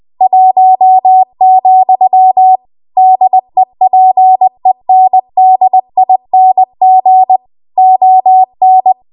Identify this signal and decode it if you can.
Morse code, heard as a keyed tone, about 20 wpm, 750 Hz. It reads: 1, DEPENDING ON
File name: Archive / 20 WPM / 20WPM